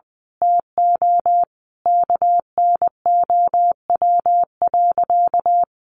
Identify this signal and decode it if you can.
Morse code, heard as a keyed tone, about 20 wpm, 700 Hz. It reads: TO KNOW.
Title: Sentences from Top 1200 Words 20wpm